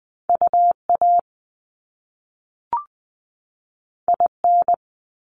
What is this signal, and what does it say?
Morse code, heard as a keyed tone, about 20 wpm, 700 Hz. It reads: UA IN
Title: Top Two Letter Combinations in Words 20wpm